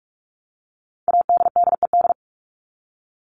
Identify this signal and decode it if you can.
Morse code, heard as a keyed tone, about 44 wpm, 700 Hz. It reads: ADDED
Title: ED Suffix - Top 100 Words with Base Words 45wpm